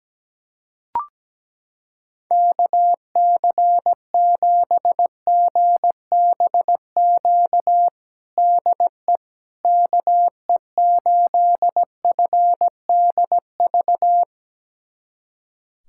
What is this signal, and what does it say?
Morse code, heard as a keyed tone, about 17 wpm, 700 Hz. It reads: KC7GBQ DE KE8FDV
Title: USA Callsign DE Callsign - 6 Characters - 17wpm